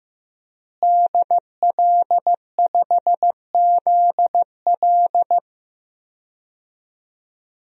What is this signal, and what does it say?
Morse code, heard as a keyed tone, about 15 wpm, 700 Hz. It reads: DL5ZL
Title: DXCC-100 Call Signs - 5 Characters - Repeated 3x - 15wpm